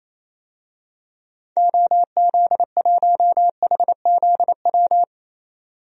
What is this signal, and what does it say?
Morse code, heard as a keyed tone, about 28 wpm, 700 Hz. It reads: OZ1HZW
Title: DXCC-100 Call Signs - 6 Characters - 28wpm